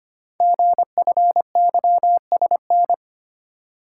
Morse code, heard as a keyed tone, about 25 wpm, 700 Hz. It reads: GFYSN
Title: Code Groups - 5 Letter:Numbers - 25wpm